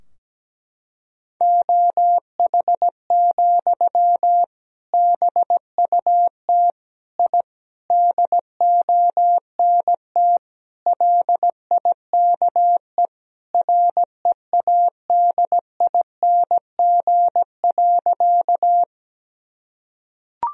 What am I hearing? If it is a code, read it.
Morse code, heard as a keyed tone, about 17 wpm, 700 Hz. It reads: OH, BUT I DONT LIKE READING.
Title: Sentences from Top 1000 Words - Review Part 3 - 17wpm